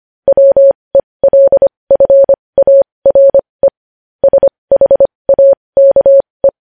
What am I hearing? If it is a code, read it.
Morse code, heard as a keyed tone, about 25 wpm, spaced as for 20 wpm, 550 Hz. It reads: WELFARE SHAKE
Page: news-headlines-in-morse-code-at-20-wpm